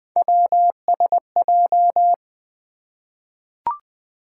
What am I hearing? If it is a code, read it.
Morse code, heard as a keyed tone, about 20 wpm, 700 Hz. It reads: WSJ
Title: Three Letter-Number Combinations - 20wpm